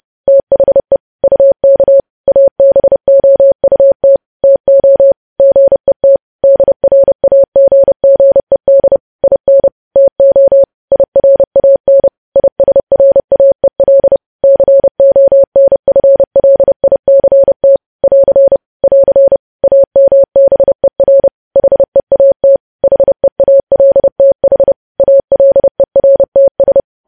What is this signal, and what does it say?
Morse code, heard as a keyed tone, about 30 wpm, 550 Hz. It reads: THE UK ABOUT TO GET DRAGGED IN TO IRAN ISRAEL CONFLICT <AR> <AR> AMBER HEAT HEALTH ALERTS
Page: news headlines in morse code at 30 wpm